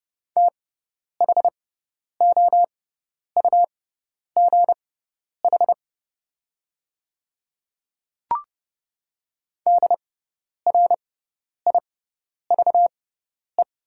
Morse code, heard as a keyed tone, about 30 wpm, spaced as for 10 wpm, 700 Hz. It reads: THOUGH DRIVE
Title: 6x Character Spacing - Top 500 Words - 30wpm